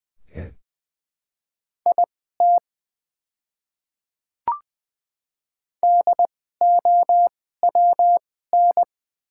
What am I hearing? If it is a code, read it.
Morse code, heard as a keyed tone, about 20 wpm, spaced as for 15 wpm, 700 Hz. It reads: IT DOWN
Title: 2x Character Spacing - Top 200 Words - 20wpm